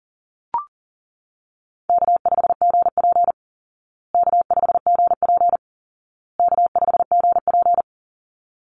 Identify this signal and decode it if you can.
Morse code, heard as a keyed tone, about 40 wpm, 700 Hz. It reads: K5GP K5GP K5GP
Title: Call Signs Repeated 2x - 40wpm